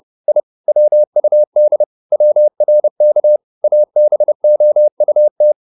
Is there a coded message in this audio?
Yes: Morse code from 0.3 to 5.5 s, heard as a keyed tone, about 30 wpm, 600 Hz. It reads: I WUD WRK ABOUT